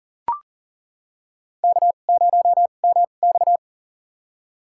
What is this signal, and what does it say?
Morse code, heard as a keyed tone, about 40 wpm, spaced as for 29 wpm, 700 Hz. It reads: K0MX
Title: USA Call Signs - 2x Character Spacing - 40wpm